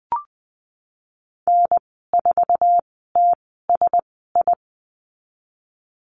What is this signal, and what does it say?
Morse code, heard as a keyed tone, about 20 wpm, spaced as for 15 wpm, 700 Hz. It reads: N4TSI
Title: 055 Lesson 10g - Letter r - Call Signs 20wpm